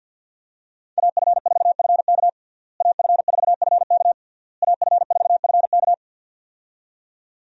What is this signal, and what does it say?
Morse code, heard as a keyed tone, about 47 wpm, 700 Hz. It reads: AF4FK AF4FK AF4FK
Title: USA Call Signs - 5 Characters - Repeated 3x - 50wpm